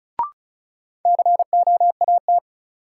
Morse code, heard as a keyed tone, about 36 wpm, 700 Hz. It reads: COAT
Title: Sentences from Top 1600 Words - New Word Drill Alt - 35wpm